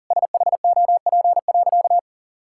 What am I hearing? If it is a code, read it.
Morse code, heard as a keyed tone, about 40 wpm, 700 Hz. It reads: SHOP.
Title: Sentences from Top 1600 Words 40wpm